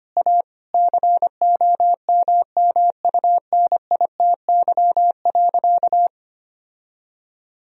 Morse code, heard as a keyed tone, about 25 wpm, 700 Hz. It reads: A COMMUNITY.
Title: Sentences from Top 1000 Words - Review Part 4 - 25wpm